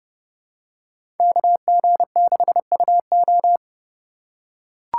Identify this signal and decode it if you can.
Morse code, heard as a keyed tone, about 30 wpm, 700 Hz. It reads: KG6UO
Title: USA Call Signs - 5 Characters - 4x Character Spacing - 30wpm